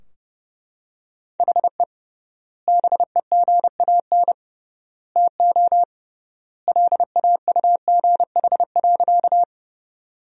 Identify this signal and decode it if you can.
Morse code, heard as a keyed tone, about 30 wpm, 700 Hz. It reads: HE BEGAN TO LAUGH.
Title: 3x Word-Spacing - Sentences from Top 500 Words - 30wpm